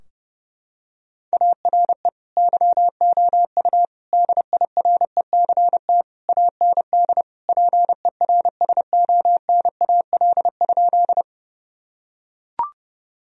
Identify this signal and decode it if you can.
Morse code, heard as a keyed tone, about 30 wpm, 700 Hz. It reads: ARE YOU DIRECT AND PERSONAL?
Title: Sentences from Top 1000 Words - Review Part 4 - 30wpm